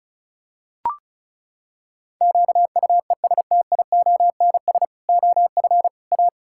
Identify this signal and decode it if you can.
Morse code, heard as a keyed tone, about 36 wpm, 700 Hz. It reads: QUESTIONS OF A
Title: Pangram Challenge v2 - 35wpm